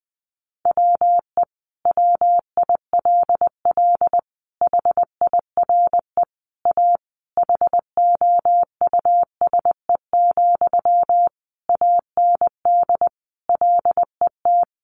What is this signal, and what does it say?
Morse code, heard as a keyed tone, about 19 wpm, 700 Hz. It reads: WE WILL HIRE A HOUSE, AND LET
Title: Sentences from Top 1400 Words 20wpm